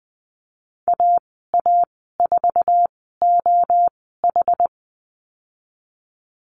Morse code, heard as a keyed tone, about 20 wpm, spaced as for 14 wpm, 700 Hz. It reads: AA4OH
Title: 082 Lesson 14g - Number 2 - Call Signs 20wpm